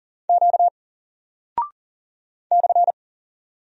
Morse code, heard as a keyed tone, about 40 wpm, 700 Hz. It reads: Q /
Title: Single Character (Letter:Number:Punctuation:Prosign) 40wpm